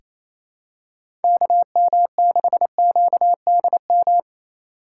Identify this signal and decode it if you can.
Morse code, heard as a keyed tone, about 28 wpm, 700 Hz. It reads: KM6QDM
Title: USA Call Signs - 6 Characters - 3x Character Spacing - 28wpm